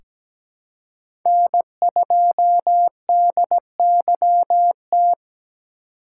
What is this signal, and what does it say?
Morse code, heard as a keyed tone, about 17 wpm, 700 Hz. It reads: N2DYT
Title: K1USN SST QSO Practice - 17wpm